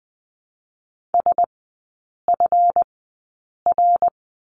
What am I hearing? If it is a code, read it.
Morse code, heard as a keyed tone, about 20 wpm, spaced as for 8 wpm, 700 Hz. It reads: SFR
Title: 138 Lesson 23e - Letter f - Three Characters 20wpm